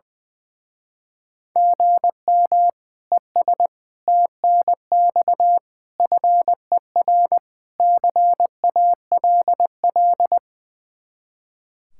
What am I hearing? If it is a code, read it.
Morse code, heard as a keyed tone, about 21 wpm, 700 Hz. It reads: GM ES TNX FER CALL
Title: Speed-Racing Common QSO Phrases - 20wpm